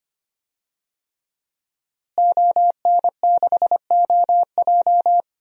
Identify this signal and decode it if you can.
Morse code, heard as a keyed tone, about 25 wpm, 700 Hz. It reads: ON6OJ
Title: DXCC-100 Call Signs - 5 Characters - Repeated 3x - 25wpm